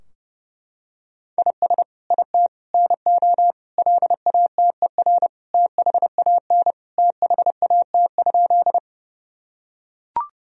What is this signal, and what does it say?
Morse code, heard as a keyed tone, about 30 wpm, 700 Hz. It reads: IS IT NO LATER THAN THAT?
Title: Sentences from Top 1000 Words - Review Part 4 - 30wpm